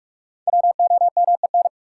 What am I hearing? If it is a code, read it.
Morse code, heard as a keyed tone, about 43 wpm, 700 Hz. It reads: WOMEN
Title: Sets of 3 Words - New Word Drill - 45wpm